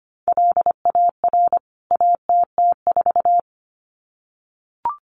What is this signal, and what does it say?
Morse code, heard as a keyed tone, about 25 wpm, 700 Hz. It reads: LAR ATT4
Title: CWT Exchange 25wpm